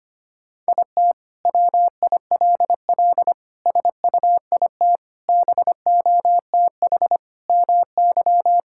Morse code, heard as a keyed tone, about 25 wpm, 700 Hz. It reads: IT WILL SUIT BOTH MY
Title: Sentences from Top 1600 Words 25wpm